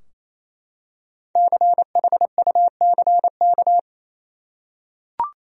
Morse code, heard as a keyed tone, about 28 wpm, 700 Hz. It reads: CHUCK
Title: USA Common Names - 28wpm